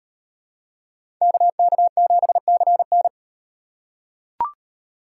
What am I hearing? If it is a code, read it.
Morse code, heard as a keyed tone, about 38 wpm, 700 Hz. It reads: KK7CN